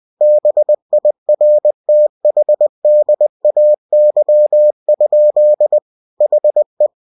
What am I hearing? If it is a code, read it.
Morse code, heard as a keyed tone, about 20 wpm, 600 Hz. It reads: BIRTHDAY? HE
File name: cw-books / winnie-the-pooh / winnie-20wpm-page-0047